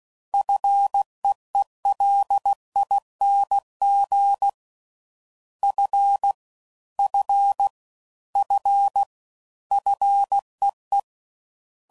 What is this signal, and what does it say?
Morse code, heard as a keyed tone, about 16 wpm, 800 Hz. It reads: FEELING F F F FEE